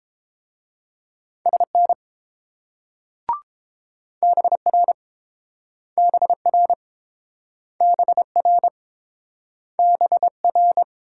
Morse code, heard as a keyed tone, about 28 wpm, 700 Hz. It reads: SN BR BR BR BR